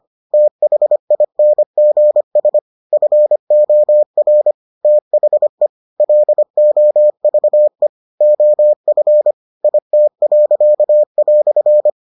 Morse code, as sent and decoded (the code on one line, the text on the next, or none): - .... .. -. --. ... / ..-. --- .-. / - .... / .-.. --- ...- . / --- ..-. / .. - .-.-.- .-..-.
THINGS FOR THE LOVE OF IT."